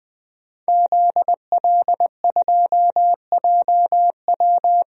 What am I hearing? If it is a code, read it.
Morse code, heard as a keyed tone, about 20 wpm, 700 Hz. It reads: ZL2JW
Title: DXCC-100 Call Signs - 5 Characters - 20wpm